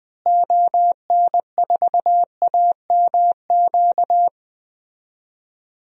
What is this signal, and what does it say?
Morse code, heard as a keyed tone, about 20 wpm, 700 Hz. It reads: ON4AMQ
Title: DXCC-100 Call Signs - 6 Characters - Repeated 2x - 20wpm